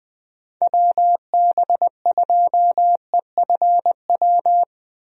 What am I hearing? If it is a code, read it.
Morse code, heard as a keyed tone, about 20 wpm, 700 Hz. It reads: WB2EFW
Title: USA Call Signs - 6 Characters - Repeated 3x - 20wpm